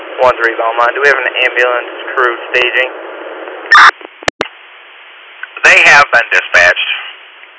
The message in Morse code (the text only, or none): none